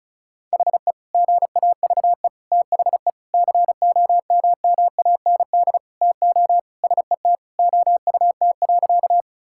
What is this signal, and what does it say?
Morse code, heard as a keyed tone, about 36 wpm, 700 Hz. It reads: HE GAVE THE COMMAND TO SET OUT.
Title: Sentences from Top 1600 Words 35wpm